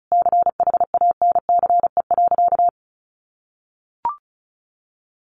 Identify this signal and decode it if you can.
Morse code, heard as a keyed tone, about 34 wpm, 700 Hz. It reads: CHANCE.
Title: Sentences from Top 1100 Words 35wpm